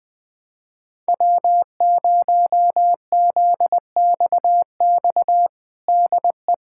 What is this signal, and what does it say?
Morse code, heard as a keyed tone, about 20 wpm, 700 Hz. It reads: W0ZXX DE
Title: USA Callsign DE Callsign - 5 Characters - 20wpm